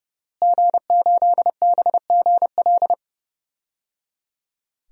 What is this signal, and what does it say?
Morse code, heard as a keyed tone, about 30 wpm, 700 Hz. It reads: G8BGL
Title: DXCC-100 Call Signs - 5 Characters - Repeated 3x - 30wpm